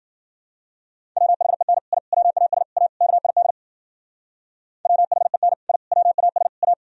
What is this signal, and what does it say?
Morse code, heard as a keyed tone, about 54 wpm, 700 Hz. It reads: WHEN I WAS A KID WHEN I WAS A